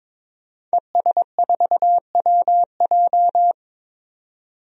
Morse code, heard as a keyed tone, about 22 wpm, 700 Hz. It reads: ES4WJ
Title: DXCC-100 Call Signs - 5 Characters - 22wpm